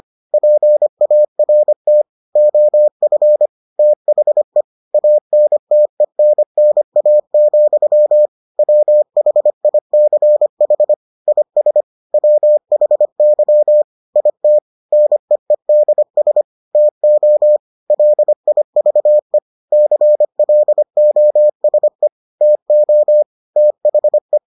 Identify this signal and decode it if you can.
Morse code, heard as a keyed tone, about 25 wpm, 600 Hz. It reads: PART OF THE ANTENNA, WHICH IS WHY IT NEEDS TO LIVE CLOSE TO THE